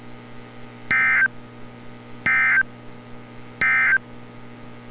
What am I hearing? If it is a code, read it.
Emergency Alert System (SAME) tones, heard as harsh buzzing bursts. It is heard from 0.8 to 4.1 s.
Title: NOAA Weather Radio - Altoona, PA